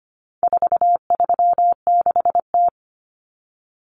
Morse code, heard as a keyed tone, about 25 wpm, 700 Hz. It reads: 436T
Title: Common Cut Numbers - 4-Digit 25wpm